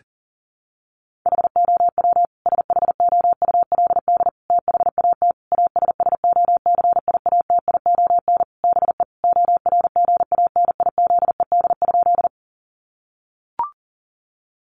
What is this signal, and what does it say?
Morse code, heard as a keyed tone, about 40 wpm, 700 Hz. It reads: HOW SHOULD THAT ASSOCIATION BE ORGANIZED?
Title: Sentences from Top 1300 Words 40wpm